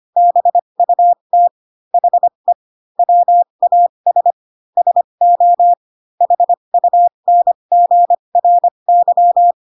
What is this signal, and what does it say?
Morse code, heard as a keyed tone, about 25 wpm, spaced as for 22 wpm, 700 Hz. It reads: BUT HE WAS SO HUNGRY
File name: Pinocchio / Pinocchio ch6 18 25 ik1hge